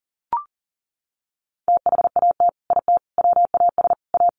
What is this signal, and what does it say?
Morse code, heard as a keyed tone, about 40 wpm, 700 Hz. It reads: THAT IT WAS A
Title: Sentences from Top 1200 Words 40wpm